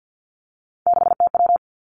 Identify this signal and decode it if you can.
Morse code, heard as a keyed tone, about 50 wpm, 700 Hz. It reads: 6TW
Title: Three Challenging Letter-Number Combinations 50wpm